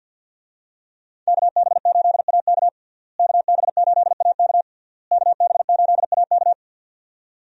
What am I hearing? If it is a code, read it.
Morse code, heard as a keyed tone, about 51 wpm, 700 Hz. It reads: KB8AK KB8AK KB8AK
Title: USA Call Signs - 5 Characters - Repeated 3x - 50wpm